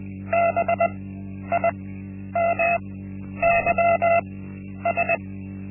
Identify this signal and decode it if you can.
Morse code, heard as a keyed tone, about 20 wpm, spaced as for 10 wpm, 680 Hz. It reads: BIMYS
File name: Linksmakalnis-1